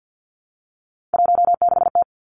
Morse code, heard as a keyed tone, about 50 wpm, 700 Hz. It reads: 16T
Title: Three Challenging Letter-Number Combinations 50wpm